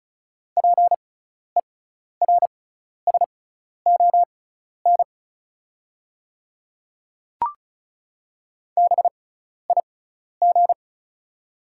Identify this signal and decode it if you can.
Morse code, heard as a keyed tone, about 36 wpm, spaced as for 12 wpm, 700 Hz. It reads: PERSON BIG
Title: 6x Character Spacing - Top 500 Words - 35wpm